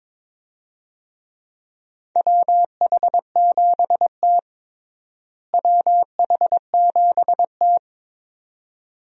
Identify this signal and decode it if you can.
Morse code, heard as a keyed tone, about 22 wpm, 700 Hz. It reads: WH7T WH7T
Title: USA Call Signs - 4 Characters - Repeated 2x - 22wpm